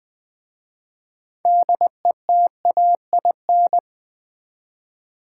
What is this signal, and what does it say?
Morse code, heard as a keyed tone, about 20 wpm, 700 Hz. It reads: DETAIN
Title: 6 Letter Words - First 3 Letters Given - 20wpm